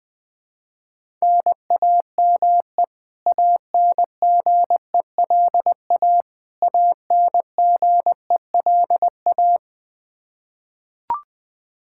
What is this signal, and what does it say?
Morse code, heard as a keyed tone, about 20 wpm, 700 Hz. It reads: NAME ANGELA ANGELA
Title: Practice QSO - Exchange 1 - Beginning - 20wpm